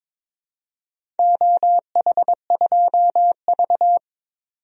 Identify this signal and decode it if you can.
Morse code, heard as a keyed tone, about 22 wpm, 700 Hz. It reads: OH2V
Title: DXCC-100 Call Signs - 4 Characters - 22wpm